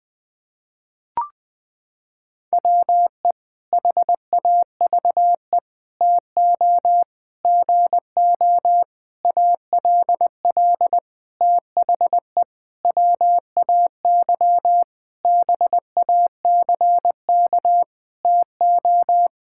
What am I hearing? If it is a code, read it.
Morse code, heard as a keyed tone, about 20 wpm, 700 Hz. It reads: WE HAVE TO GO ALL THE WAY BACK TO
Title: Sentences with 8-Word Phrases - 20wpm